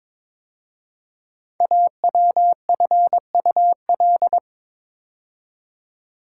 Morse code, heard as a keyed tone, about 22 wpm, 700 Hz. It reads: AWFUL
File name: 5 Letter Words - First 3 Letters Given - 22wpm